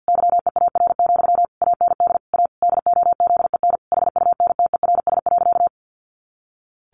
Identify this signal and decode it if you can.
Morse code, heard as a keyed tone, about 50 wpm, 700 Hz. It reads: YEAR, AND A DOZEN HUNTERS.
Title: A Sound Of Thunder 50wpm